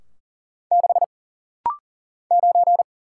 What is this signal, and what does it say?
Morse code, heard as a keyed tone, about 40 wpm, 700 Hz. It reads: 6 9
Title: Single Letter-Number Alt - Rapid-Fire 40wpm